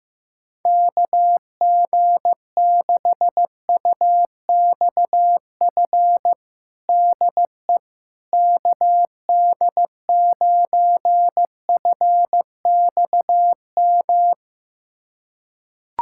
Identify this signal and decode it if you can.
Morse code, heard as a keyed tone, about 16 wpm, 700 Hz. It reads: KG6UXF DE KD9FXM
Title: USA Callsign DE Callsign 15wpm